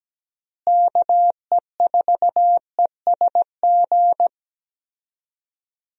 Morse code, heard as a keyed tone, about 17 wpm, 700 Hz. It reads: KE4ESG
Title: USA Call Signs - 6 Characters - Repeated 3x - 17wpm